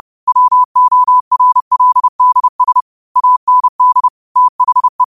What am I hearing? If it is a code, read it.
Morse code, heard as a keyed tone, about 30 wpm, 1000 Hz. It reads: WORLDS AND THE
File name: qotdfiles / 30wpm